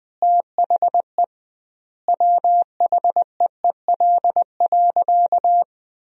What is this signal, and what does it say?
Morse code, heard as a keyed tone, about 20 wpm, 700 Hz. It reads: THE WHEEL.
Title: 2x Word-Spacing - Sentences from Top 500 Words - 20wpm